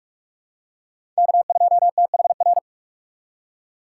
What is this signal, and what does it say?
Morse code, heard as a keyed tone, about 45 wpm, 700 Hz. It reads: K2THR
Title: USA Call Signs - 5 Characters - 6x Character Spacing - 45wpm